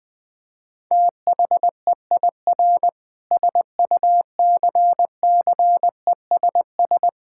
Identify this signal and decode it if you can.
Morse code, heard as a keyed tone, about 20 wpm, 700 Hz. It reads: THEIR SUCCESS
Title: Sentences from Top 1000 Words - Review Part 6 - 20wpm